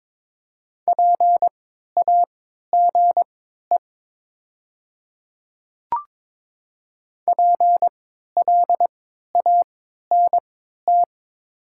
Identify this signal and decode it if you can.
Morse code, heard as a keyed tone, about 22 wpm, spaced as for 12 wpm, 690 Hz. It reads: PAGE PLANT
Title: 3x Character Spacing - Top 500 Words - 22wpm